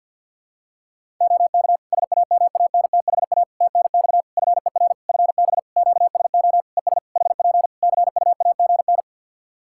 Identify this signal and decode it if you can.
Morse code, heard as a keyed tone, about 52 wpm, 700 Hz. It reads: OK SAMANTHA TNX FER FB QSO ES HP CUAGN
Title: Practice QSO - Exchange 4 - Ending - 50wpm